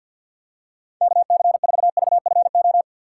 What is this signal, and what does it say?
Morse code, heard as a keyed tone, about 52 wpm, 700 Hz. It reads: KC4VFO
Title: USA Call Signs - 2x Character Spacing - 50wpm